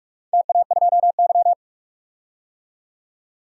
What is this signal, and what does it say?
Morse code, heard as a keyed tone, about 42 wpm, 700 Hz. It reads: TA2Y
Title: DXCC-100 Call Signs - 4 Characters - 45wpm